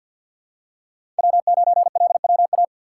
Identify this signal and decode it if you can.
Morse code, heard as a keyed tone, about 52 wpm, 700 Hz. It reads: W9LWA
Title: USA Call Signs - 4x Character Spacing - 50wpm